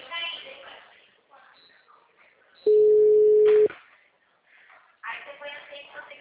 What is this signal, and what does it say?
Ringback tone (Europe, 425 Hz), with speech.
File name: out- -2012- - - .40979